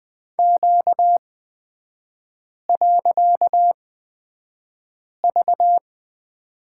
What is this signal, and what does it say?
Morse code, heard as a keyed tone, about 20 wpm, spaced as for 5 wpm, 700 Hz. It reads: Q.V
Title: 255 Lesson 41e - Period - Three Characters 20wpm